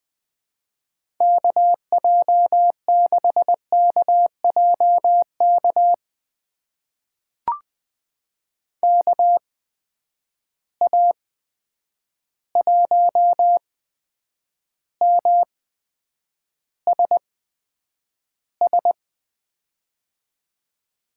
Morse code, heard as a keyed tone, about 20 wpm, spaced as for 9 wpm, 700 Hz. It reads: KJ6KJK KA1MSS
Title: USA Call Signs - 6 Characters - 8x Character Spacing - 20wpm